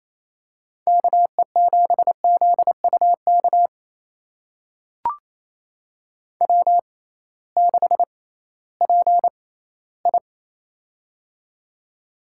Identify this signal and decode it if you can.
Morse code, heard as a keyed tone, about 28 wpm, spaced as for 16 wpm, 700 Hz. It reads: KE7ZUK W6PI
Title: USA Call Signs - 6x Character Spacing - 28wpm